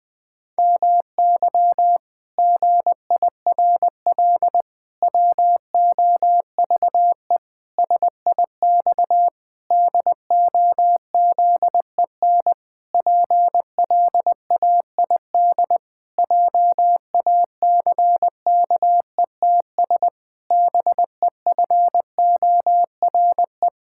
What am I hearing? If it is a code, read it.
Morse code, heard as a keyed tone, about 20 wpm, 700 Hz. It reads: MY GIRL WOVE SIX DOZEN PLAID JACKETS BEFORE
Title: Pangram Challenge v2 - 20wpm